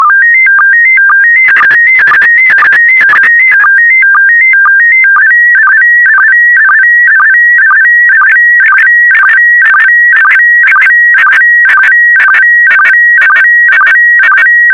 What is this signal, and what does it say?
SSTV (slow-scan television), PD 120.
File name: sstv / PD120-VK4MSL